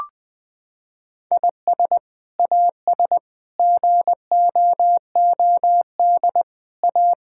Morse code, heard as a keyed tone, about 21 wpm, 700 Hz. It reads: IS AS GOOD A
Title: Sets of 8 Words - 20wpm